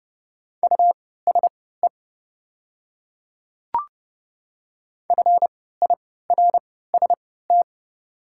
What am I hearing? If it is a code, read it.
Morse code, heard as a keyed tone, about 32 wpm, spaced as for 17 wpm, 700 Hz. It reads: USE FIRST
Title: 3x Character Spacing - Top 100 Words - 30wpm